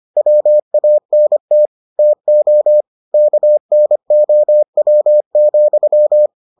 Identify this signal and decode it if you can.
Morse code, heard as a keyed tone, about 25 wpm, 600 Hz. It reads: WANT TO KNOW,